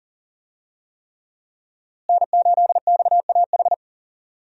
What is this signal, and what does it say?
Morse code, heard as a keyed tone, about 40 wpm, 700 Hz. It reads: N8XAH
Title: USA Call Signs - 5 Characters - Repeated 3x - 40wpm